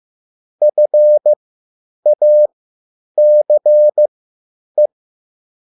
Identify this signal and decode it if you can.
Morse code, heard as a keyed tone, about 15 wpm, spaced as for 9 wpm, 600 Hz. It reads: FACE